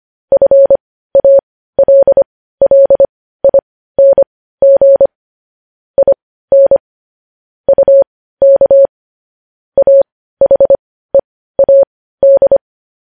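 Morse code, heard as a keyed tone, about 25 wpm, spaced as for 15 wpm, 550 Hz. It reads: FALLING IN UK AHEAD
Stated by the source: News headlines sent in Morse code at 15 wpm.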